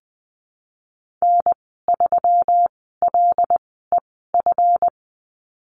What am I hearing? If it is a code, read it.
Morse code, heard as a keyed tone, about 20 wpm, spaced as for 15 wpm, 700 Hz. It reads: N3LEF
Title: 140 Lesson 23g - Letter f - Call Signs 20wpm